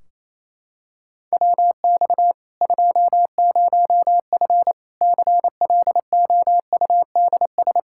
Morse code, heard as a keyed tone, about 28 wpm, 700 Hz. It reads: WX 20F CLOUDS
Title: Practice QSO - Exchange 2 - Info - 28wpm